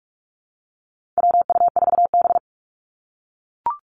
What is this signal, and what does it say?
Morse code, heard as a keyed tone, about 44 wpm, 700 Hz. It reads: WU4B
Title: Call Signs - 45wpm